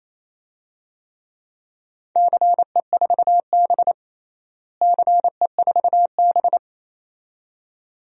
Morse code, heard as a keyed tone, about 27 wpm, 700 Hz. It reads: CE4B CE4B
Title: DXCC-100 Call Signs - 4 Characters - Repeated 2x - 28wpm